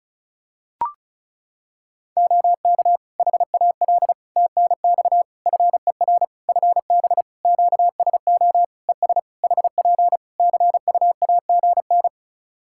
Morse code, heard as a keyed tone, about 36 wpm, 700 Hz. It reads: OK HAL TNX FER FB QSO ES HP CUAGN
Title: Practice QSO - Exchange 4 - Ending - 35wpm